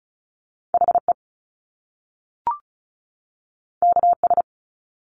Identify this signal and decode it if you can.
Morse code, heard as a keyed tone, about 36 wpm, 700 Hz. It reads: HE KS
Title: Top Two Letter Combinations in Words 35wpm